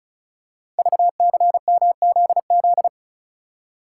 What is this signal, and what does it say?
Morse code, heard as a keyed tone, about 35 wpm, 700 Hz. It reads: VCMZZ
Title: Code Groups - 5 Letters - 35wpm